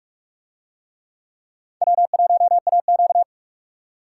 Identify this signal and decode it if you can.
Morse code, heard as a keyed tone, about 44 wpm, 700 Hz. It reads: W1AQ